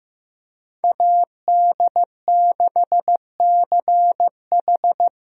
Morse code, heard as a keyed tone, about 15 wpm, 700 Hz. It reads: AD6CH
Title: USA Call Signs - 5 Characters - Repeated 2x - 15wpm